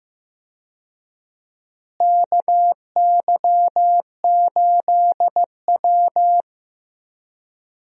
Morse code, heard as a keyed tone, about 15 wpm, 700 Hz. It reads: KY8W
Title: USA Call Signs - 4 Characters - Repeated 3x - 15wpm